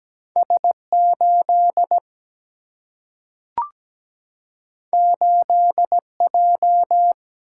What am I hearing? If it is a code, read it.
Morse code, heard as a keyed tone, about 17 wpm, 700 Hz. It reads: S8 8J
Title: Single-Letter Single-Number 17wpm